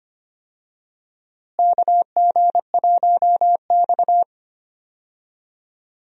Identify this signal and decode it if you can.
Morse code, heard as a keyed tone, about 25 wpm, 700 Hz. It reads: KG1X
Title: USA Call Signs - 4 Characters - Repeated 2x - 25wpm